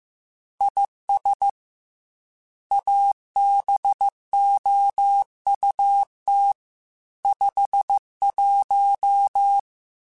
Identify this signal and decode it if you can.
Morse code, heard as a keyed tone, about 15 wpm, 800 Hz. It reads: IS ABOUT 51